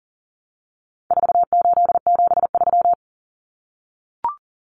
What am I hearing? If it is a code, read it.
Morse code, heard as a keyed tone, about 40 wpm, 700 Hz. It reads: 4873
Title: Common Cut Numbers - 4-Digit 40wpm